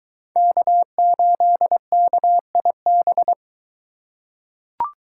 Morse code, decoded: K8KIB

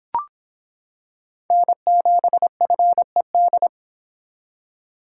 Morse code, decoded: N7FED

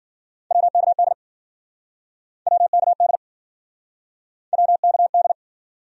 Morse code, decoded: WKD WKD WKD

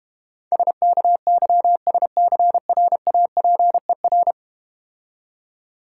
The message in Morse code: SKYSCRAPER

